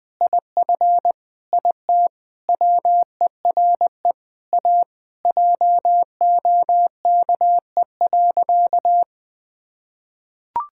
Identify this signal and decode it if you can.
Morse code, heard as a keyed tone, about 19 wpm, 700 Hz. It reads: IF IT WERE A JOKE.
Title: Sentences with 5-Word Phrases - 20wpm